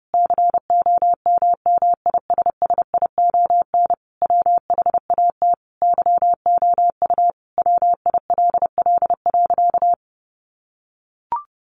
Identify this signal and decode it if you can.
Morse code, heard as a keyed tone, about 30 wpm, 700 Hz. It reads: COMMISSION WHAT YOU WILL.
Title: Sentences from Top 1400 Words 30wpm